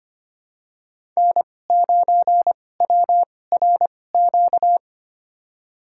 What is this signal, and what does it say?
Morse code, heard as a keyed tone, about 25 wpm, spaced as for 18 wpm, 700 Hz. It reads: N9WRQ
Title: USA Call Signs - 5 Characters - 2x Character Spacing - 25wpm